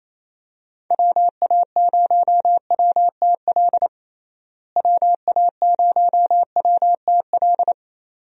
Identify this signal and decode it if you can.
Morse code, heard as a keyed tone, about 28 wpm, 700 Hz. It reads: WA0WTL WA0WTL
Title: USA Call Signs - 6 Characters - Repeated 2x - 28wpm